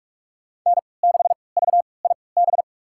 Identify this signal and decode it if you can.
Morse code, heard as a keyed tone, about 46 wpm, spaced as for 26 wpm, 700 Hz. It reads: N6VIB
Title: USA Call Signs - 5 Characters - 3x Character Spacing - 45wpm